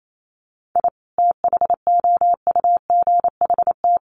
Morse code, heard as a keyed tone, about 28 wpm, 700 Hz. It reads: I THOUGHT
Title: Sets of 2 Words 28wpm